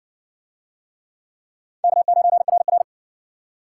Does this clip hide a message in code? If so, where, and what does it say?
Yes: Morse code from 1.8 to 2.8 s, heard as a keyed tone, about 60 wpm, 700 Hz. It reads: K9RR